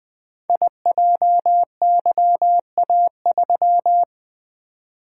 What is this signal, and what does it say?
Morse code, heard as a keyed tone, about 20 wpm, 700 Hz. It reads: IJYA3
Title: Code Groups - 5 Letter:Numbers - 20wpm